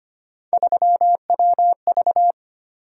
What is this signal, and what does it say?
Morse code, heard as a keyed tone, about 25 wpm, 700 Hz. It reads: 3WV
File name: Three Letter-Number Combinations - 25wpm